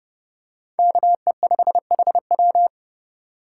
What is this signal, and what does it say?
Morse code, heard as a keyed tone, about 29 wpm, 700 Hz. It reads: KE5HW